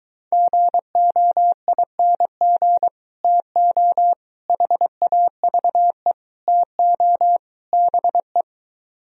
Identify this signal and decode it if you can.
Morse code, heard as a keyed tone, about 23 wpm, 700 Hz. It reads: GOING TO HAVE TO BE